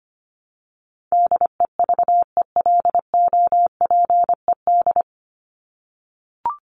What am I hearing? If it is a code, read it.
Morse code, heard as a keyed tone, about 25 wpm, 700 Hz. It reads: DEVELOPED
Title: ED Suffix - Top 100 Words with Base Words 25wpm